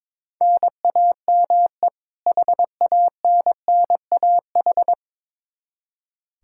Morse code, heard as a keyed tone, about 22 wpm, 700 Hz. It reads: NAME HANNAH
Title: Practice QSO - Exchange 1 - Beginning - 22wpm